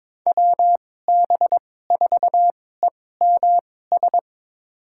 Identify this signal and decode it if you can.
Morse code, heard as a keyed tone, about 23 wpm, spaced as for 16 wpm, 700 Hz. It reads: WB4EMS
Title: USA Call Signs - 2x Character Spacing - 22wpm